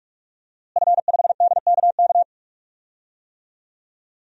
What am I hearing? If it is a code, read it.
Morse code, heard as a keyed tone, about 46 wpm, 700 Hz. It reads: F5DKK
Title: DXCC-100 Call Signs - 5 Characters - Repeated 3x - 45wpm